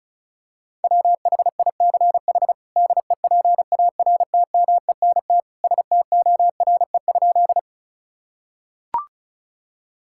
Morse code, heard as a keyed tone, about 36 wpm, 700 Hz. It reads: WHICH DEPARTMENT STORE?
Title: Sentences from Top 1000 Words - Review Part 5 - 35wpm